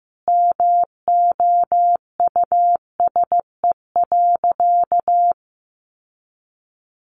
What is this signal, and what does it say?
Morse code, heard as a keyed tone, about 15 wpm, 700 Hz. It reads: MOUSE.
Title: Sentences from Top 1500 Words 15wpm